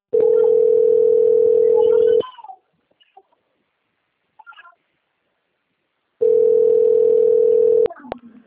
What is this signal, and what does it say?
Ringback tone (North America), 440 + 480 Hz.